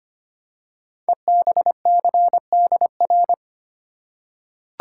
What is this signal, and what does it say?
Morse code, heard as a keyed tone, about 25 wpm, 700 Hz. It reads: EBCDR
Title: Code Groups - 5 Letters - 25wpm